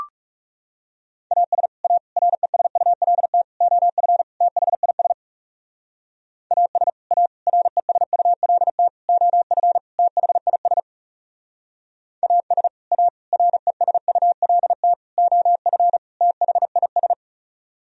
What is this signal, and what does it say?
Morse code, heard as a keyed tone, about 34 wpm, 700 Hz. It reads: AS A RESULT OF THIS AS A RESULT OF THIS AS A RESULT OF THIS
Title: Speed-Racing - Sets of 5 Words 30wpm